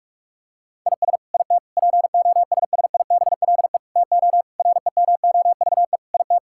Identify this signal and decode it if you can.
Morse code, heard as a keyed tone, about 46 wpm, 700 Hz. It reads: IS IT POSSIBLE TO REMOVE IT
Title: Sentences from Top 1000 Words - Review Part 5 - 45wpm